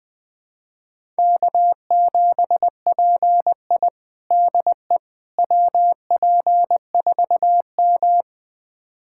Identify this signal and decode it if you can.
Morse code, heard as a keyed tone, about 21 wpm, 700 Hz. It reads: K7PI DE WP4M
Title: USA Callsign DE Callsign - 4 Characters - 20wpm